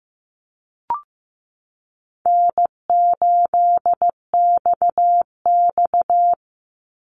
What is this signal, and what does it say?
Morse code, heard as a keyed tone, about 15 wpm, 700 Hz. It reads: N8XX
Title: Easy Call Signs Repeated 2x - 15wpm